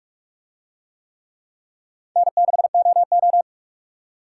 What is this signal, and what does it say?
Morse code, heard as a keyed tone, about 44 wpm, 700 Hz. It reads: N6OO